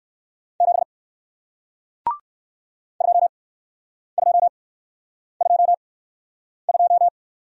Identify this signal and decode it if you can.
Morse code, heard as a keyed tone, about 54 wpm, spaced as for 9 wpm, 700 Hz. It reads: 7 2222
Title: Speed-Racing Single Letter-Number 45wpm